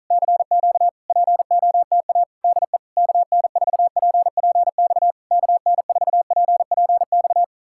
Morse code, heard as a keyed tone, about 46 wpm, 700 Hz. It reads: CQ POTA DE KN4PPX KN4PPX